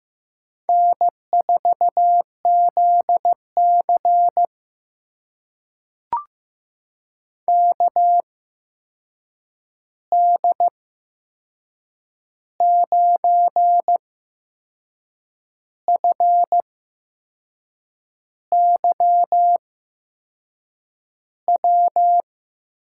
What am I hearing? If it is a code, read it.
Morse code, heard as a keyed tone, about 15 wpm, spaced as for 6 wpm, 700 Hz. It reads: N4ZC KD9FYW
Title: USA Call Signs - 8x Character Spacing - 15wpm